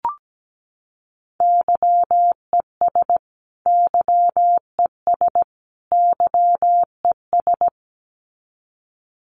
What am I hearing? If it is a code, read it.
Morse code, heard as a keyed tone, about 17 wpm, 700 Hz. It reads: YES YES YES
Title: Top 500 Words Repeated 17wpm